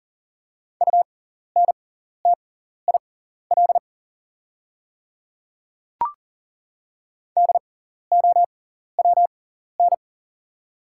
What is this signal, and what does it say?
Morse code, heard as a keyed tone, about 37 wpm, spaced as for 14 wpm, 690 Hz. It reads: UNTIL DOWN
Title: 6x Character Spacing - Top 500 Words - 40wpm